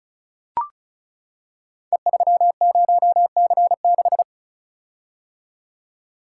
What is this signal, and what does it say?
Morse code, heard as a keyed tone, about 36 wpm, 700 Hz. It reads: E30C6
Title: Code Groups - 5 Letter:Numbers - 35wpm